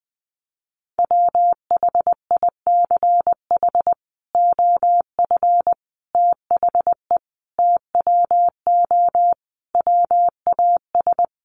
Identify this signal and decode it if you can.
Morse code, heard as a keyed tone, about 20 wpm, 700 Hz. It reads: WHICH OF THE TWO WAS
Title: Sentences from Top 1300 Words 20wpm